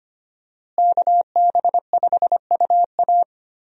Morse code, heard as a keyed tone, about 25 wpm, 700 Hz. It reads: KB5UA
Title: USA Call Signs - 5 Characters - Repeated 3x - 25wpm